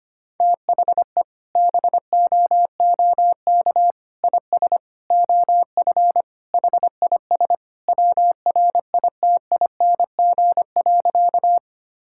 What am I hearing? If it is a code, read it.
Morse code, heard as a keyed tone, about 25 wpm, 690 Hz. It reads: THE BOOK IS OF HIS WRITING.
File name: practice / 25wpm / Sentences with 2-Word Phrases - 25wpm